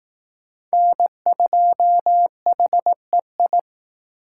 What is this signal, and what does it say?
Morse code, heard as a keyed tone, about 18 wpm, 700 Hz. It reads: N2HEI
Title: Speed Racing - USA 1x3 Call Signs 15wpm